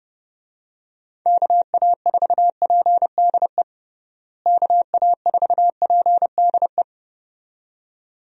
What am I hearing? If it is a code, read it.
Morse code, heard as a keyed tone, about 30 wpm, 700 Hz. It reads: KA4PDE KA4PDE
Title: USA Call Signs - 6 Characters - Repeated 2x - 30wpm